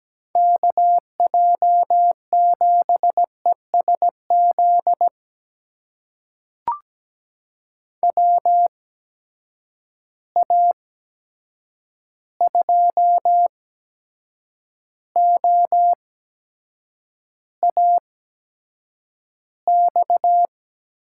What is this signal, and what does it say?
Morse code, heard as a keyed tone, about 17 wpm, spaced as for 7 wpm, 700 Hz. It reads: KJ7ESZ WA2OAX
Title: USA Call Signs - 6 Characters - 8x Character Spacing - 17wpm